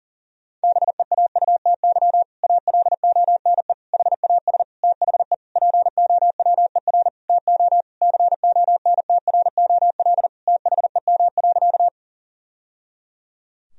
Morse code, heard as a keyed tone, about 40 wpm, 700 Hz. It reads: BEAUTY ALONE HAS THE POWER TO CONTROL THEM.
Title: Sentences from Top 1000 Words - Review Part 2 - 40wpm